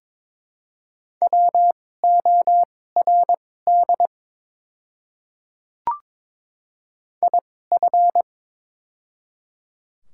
Morse code, heard as a keyed tone, about 23 wpm, spaced as for 16 wpm, 700 Hz. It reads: WORD IF
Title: 2x Character Spacing - Top 100 Words - 22wpm